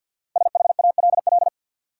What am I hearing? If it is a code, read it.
Morse code, heard as a keyed tone, about 48 wpm, 700 Hz. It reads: SHALL
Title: Five Letter Words - 50wpm